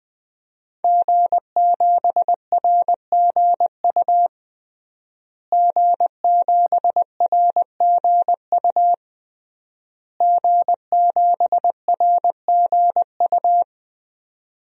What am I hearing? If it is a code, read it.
Morse code, heard as a keyed tone, about 20 wpm, 700 Hz. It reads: G7RGU G7RGU G7RGU
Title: DXCC-100 Call Signs - 5 Characters - Repeated 3x - 20wpm